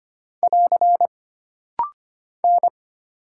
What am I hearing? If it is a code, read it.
Morse code, heard as a keyed tone, about 25 wpm, 700 Hz. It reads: <AR> N
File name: Single Character (Letter:Number:Punctuation:Prosign) - Rapid-Fire - 25wpm